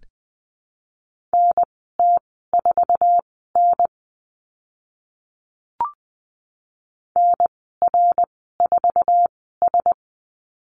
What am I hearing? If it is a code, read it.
Morse code, heard as a keyed tone, about 20 wpm, spaced as for 15 wpm, 700 Hz. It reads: NT4N NR4S
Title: 055 Lesson 10g - Letter r - Call Signs 20wpm